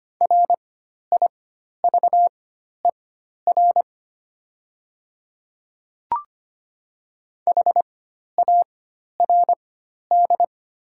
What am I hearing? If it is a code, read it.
Morse code, heard as a keyed tone, about 25 wpm, spaced as for 12 wpm, 700 Hz. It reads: RIVER HARD
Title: 4x Character Spacing - Top 500 Words - 25wpm